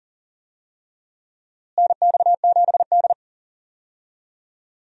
Morse code, heard as a keyed tone, about 40 wpm, 700 Hz. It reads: NX7D
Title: USA Call Signs - 4 Characters - Repeated 3x - 40wpm